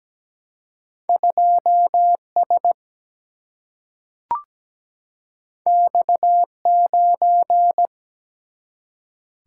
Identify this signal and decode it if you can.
Morse code, heard as a keyed tone, about 17 wpm, 700 Hz. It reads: 2 S X 9
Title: Single-Letter Single-Number 17wpm